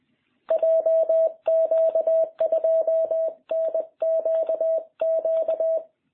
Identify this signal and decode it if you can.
Morse code, heard as a keyed tone, about 21 wpm, 660 Hz. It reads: JQ2NQQ